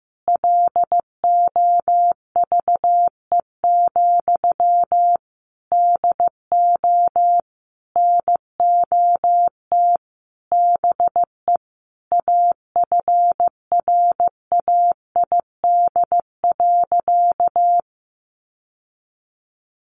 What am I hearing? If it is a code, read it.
Morse code, heard as a keyed tone, about 15 wpm, 700 Hz. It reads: LOVE, DO NOT BE AFRAID.
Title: Sentences from Top 1500 Words 15wpm